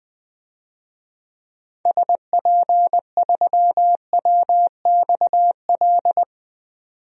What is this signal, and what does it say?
Morse code, heard as a keyed tone, about 20 wpm, 700 Hz. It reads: SP3WXL
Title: DXCC-100 Call Signs - 6 Characters - 20wpm